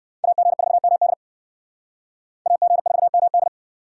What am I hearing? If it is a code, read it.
Morse code, heard as a keyed tone, about 68 wpm, 700 Hz. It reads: AG4MD AG4MD